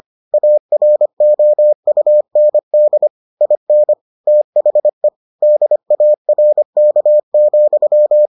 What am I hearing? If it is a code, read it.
Morse code, heard as a keyed tone, about 25 wpm, 600 Hz. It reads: AROUND IN THE DARK,